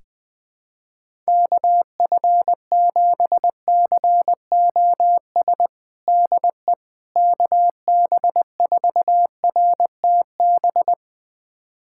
Morse code, heard as a keyed tone, about 20 wpm, 700 Hz. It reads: KF7COS DE KB4RTB
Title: USA Callsign DE Callsign - 6 Characters - 20wpm